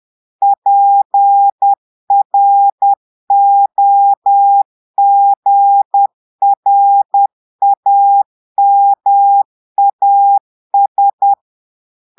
Morse code, heard as a keyed tone, about 10 wpm, 800 Hz. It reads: PROGRAMAS
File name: portugais texte en clair 2 vitesse 10 mots minutes